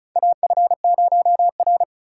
Morse code, heard as a keyed tone, about 36 wpm, 700 Hz. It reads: AF0R
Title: USA Call Signs - 4 Characters - 2x Character Spacing - 35wpm